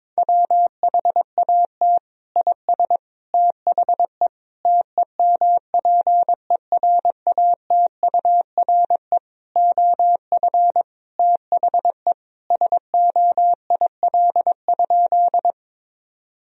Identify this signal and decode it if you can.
Morse code, heard as a keyed tone, about 22 wpm, 700 Hz. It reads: WHAT IS THE TEMPERATURE OF THE SOIL?
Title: Sentences from Top 1000 Words - Review Part 4 - 22wpm